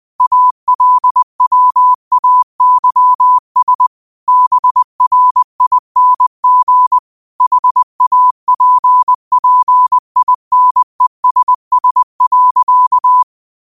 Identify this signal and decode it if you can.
Morse code, heard as a keyed tone, about 20 wpm, 1000 Hz. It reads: ALWAYS BRING HAPPINESS.